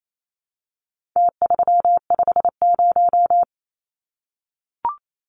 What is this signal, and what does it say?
Morse code, heard as a keyed tone, about 28 wpm, 700 Hz. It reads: T350
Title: Numbers - Contest Style - 4-Digit 28wpm